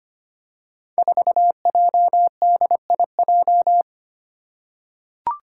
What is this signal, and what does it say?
Morse code, heard as a keyed tone, about 25 wpm, 700 Hz. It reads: 4JDIJ
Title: Five Challenging Letter-Number Combinations 25wpm